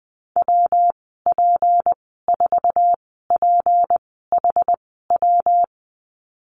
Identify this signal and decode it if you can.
Morse code, heard as a keyed tone, about 20 wpm, spaced as for 15 wpm, 700 Hz. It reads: WP4PHW
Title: 179 Lesson 29g - Slash - Call Signs 20wpm